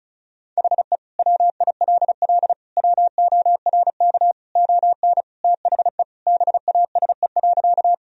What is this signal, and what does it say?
Morse code, heard as a keyed tone, about 36 wpm, 690 Hz. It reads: HE WILL WORK ON THE BASE.
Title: Sentences from Top 1000 Words - Review Part 1 - 35wpm